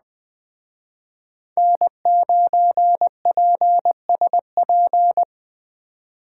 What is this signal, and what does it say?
Morse code, heard as a keyed tone, about 21 wpm, 700 Hz. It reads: N9PSP